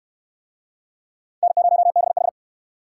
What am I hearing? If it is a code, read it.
Morse code, heard as a keyed tone, about 68 wpm, 700 Hz. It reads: N0BH